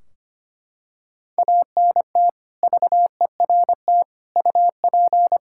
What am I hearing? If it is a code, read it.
Morse code, heard as a keyed tone, about 25 wpm, 700 Hz. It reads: ANT VERT UP
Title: Practice QSO - Exchange 2 - Info - 25wpm